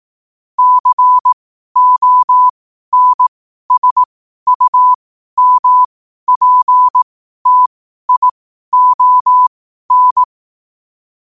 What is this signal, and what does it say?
Morse code, heard as a keyed tone, about 18 wpm, spaced as for 12 wpm, 1000 Hz. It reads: CONSUMPTION
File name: qotdfiles / 12wpm